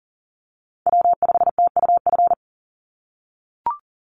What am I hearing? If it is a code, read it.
Morse code, heard as a keyed tone, about 40 wpm, 700 Hz. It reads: W5TUF
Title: Call Signs - 40wpm